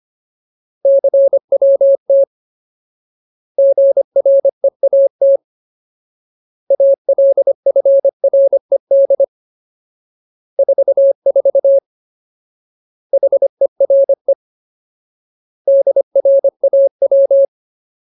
Morse code, heard as a keyed tone, about 25 wpm, 550 Hz. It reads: CWT GREAT ALFRED 44 HERE DRAW
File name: Ultimate Word List / 25WPM Ultimate